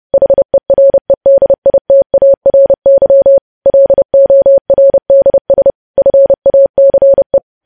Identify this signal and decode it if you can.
Morse code, heard as a keyed tone, about 30 wpm, 550 Hz. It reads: HEREDITARY LORDS FACE
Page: news headlines in morse code at 30 wpm